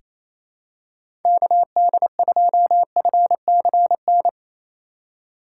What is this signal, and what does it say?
Morse code, heard as a keyed tone, about 27 wpm, 700 Hz. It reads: KD2FCN
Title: USA Call Signs - 6 Characters - 2x Character Spacing - 28wpm